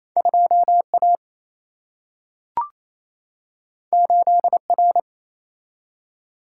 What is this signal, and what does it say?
Morse code, heard as a keyed tone, about 28 wpm, 700 Hz. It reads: 2A 8R
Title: Single-Letter Single-Number 28wpm